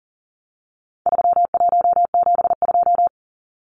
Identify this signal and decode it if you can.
Morse code, heard as a keyed tone, about 40 wpm, 700 Hz. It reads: 3172
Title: Numbers - 4-Digit 40wpm